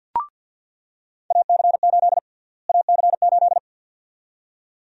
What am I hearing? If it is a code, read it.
Morse code, heard as a keyed tone, about 46 wpm, 700 Hz. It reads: AC8 AC8
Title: USA Call Sign Prefixes - Repeated 2x - 50wpm